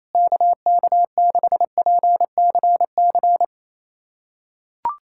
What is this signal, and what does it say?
Morse code, heard as a keyed tone, about 28 wpm, 700 Hz. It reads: KK6PCC
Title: USA Callsign DE Callsign 28wpm